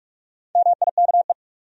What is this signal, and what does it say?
Morse code, heard as a keyed tone, about 46 wpm, 700 Hz. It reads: MIKE
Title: USA Common Names - Max 5 Letters - 45wpm